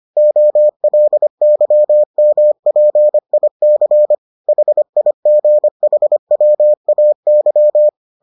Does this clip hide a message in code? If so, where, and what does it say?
Yes: Morse code from 0.2 to 7.9 s, heard as a keyed tone, about 25 wpm, 600 Hz. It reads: OLYMPIC HIGHWAY